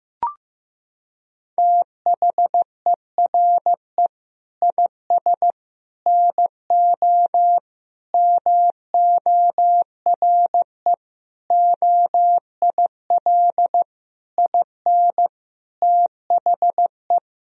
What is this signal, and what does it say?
Morse code, heard as a keyed tone, about 15 wpm, 700 Hz. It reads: THERE IS NO MORE OIL IN THE
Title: Sentences from Top 1000 Words - Review Part 4 - 15wpm